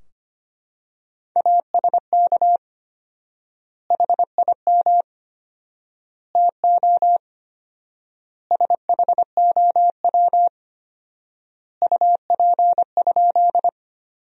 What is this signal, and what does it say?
Morse code, heard as a keyed tone, about 25 wpm, 700 Hz. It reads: ASK HIM TO SHOW UP?
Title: 4x Word-Spacing - Sentences from Top 200 Words - 25wpm